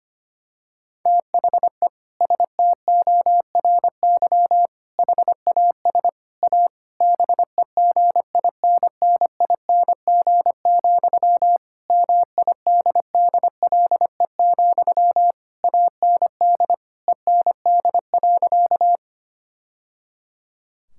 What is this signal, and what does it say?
Morse code, heard as a keyed tone, about 25 wpm, 700 Hz. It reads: THE STORY HAS A BEGINNING, MIDDLE, AND END.
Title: Sentences from Top 1000 Words - Review Part 6 - 25wpm